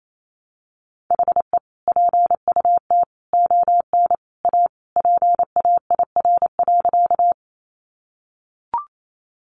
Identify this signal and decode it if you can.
Morse code, heard as a keyed tone, about 28 wpm, 700 Hz. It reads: HE PUT ON A PAIR.
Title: Sentences from Top 1200 Words 28wpm